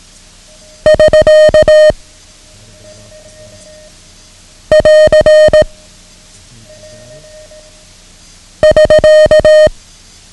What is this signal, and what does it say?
Morse code, heard as a keyed tone, about 18 wpm, 620 Hz. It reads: <SK> <AR> <SK>
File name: ham radio / CodeQuick / Code Quick 09